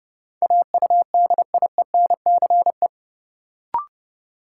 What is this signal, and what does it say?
Morse code, heard as a keyed tone, about 30 wpm, 700 Hz. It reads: AUDIENCE
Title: Sentences from Top 800 Words - New Word Drill - 30wpm